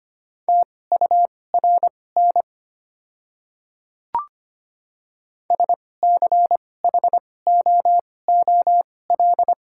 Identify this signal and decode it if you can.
Morse code, heard as a keyed tone, about 25 wpm, spaced as for 18 wpm, 700 Hz. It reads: TURN SCHOOL
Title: 2x Character Spacing - Top 300 Words - 25wpm